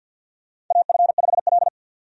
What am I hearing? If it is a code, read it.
Morse code, heard as a keyed tone, about 50 wpm, 700 Hz. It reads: AF5L
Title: USA Call Signs - 4 Characters - 50wpm